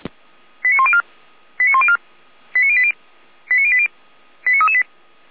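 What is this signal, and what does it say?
Five-tone selective call (ZVEI), heard as quick runs of five tones, tones of about 70 ms. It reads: 89104 89104 89080 89080 89338
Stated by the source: ZVEI five-tone call received by SDR.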